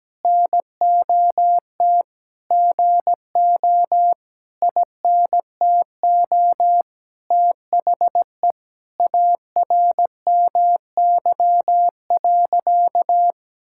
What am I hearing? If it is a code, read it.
Morse code, heard as a keyed tone, about 17 wpm, 700 Hz. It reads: NOT GO INTO THE ARMY.
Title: Sentences from Top 1000 Words - Review Part 5 - 17wpm